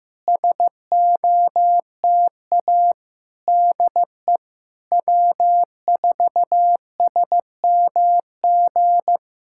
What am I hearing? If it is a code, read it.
Morse code, heard as a keyed tone, about 15 wpm, 700 Hz. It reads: SOTA DE W4SMG
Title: SOTA QSO Practice 15wpm